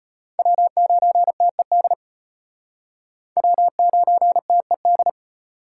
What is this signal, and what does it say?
Morse code, heard as a keyed tone, about 32 wpm, 700 Hz. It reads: W9TED W9TED